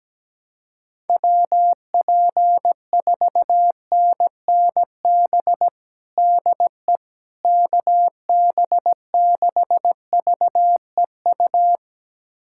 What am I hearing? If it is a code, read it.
Morse code, heard as a keyed tone, about 17 wpm, 700 Hz. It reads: WP4NNB DE KB6VEU
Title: USA Callsign DE Callsign - 6 Characters - 17wpm